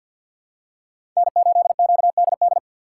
Morse code, heard as a keyed tone, about 47 wpm, 700 Hz. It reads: N8QDD
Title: USA Call Signs - 5 Characters - 2x Character Spacing - 50wpm